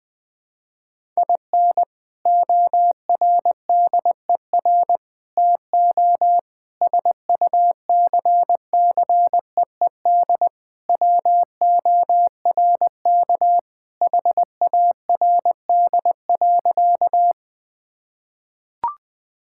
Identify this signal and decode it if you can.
Morse code, heard as a keyed tone, about 20 wpm, 700 Hz. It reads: IN ORDER TO SUCCEED WORK HARD.
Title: Sentences with 3-Word Phrases - 20wpm